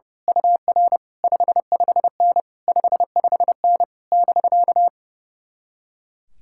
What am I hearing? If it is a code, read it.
Morse code, heard as a keyed tone, about 28 wpm, 700 Hz. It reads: UR 55N 55N <BK>
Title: Speed-Racing Common QSO Phrases - 30wpm